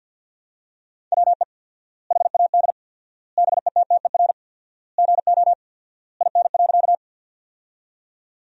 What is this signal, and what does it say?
Morse code, heard as a keyed tone, about 50 wpm, 700 Hz. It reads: WE HAD BETTER GO IN.
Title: 4x Word-Spacing - Sentences from Top 400 Words - 50wpm